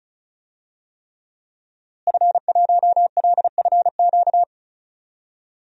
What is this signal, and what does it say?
Morse code, heard as a keyed tone, about 36 wpm, 700 Hz. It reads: F1LFQ
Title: DXCC-100 Call Signs - 5 Characters - Repeated 3x - 35wpm